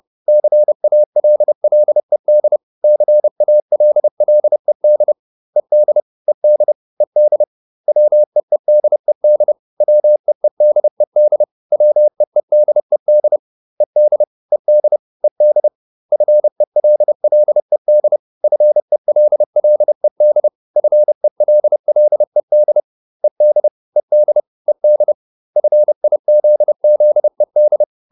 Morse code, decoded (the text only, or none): CALLED CALLED ED ED ED WEEDED WEEDED WEEDED ED ED ED FELLED FELLED FELLED ED ED ED FIZZED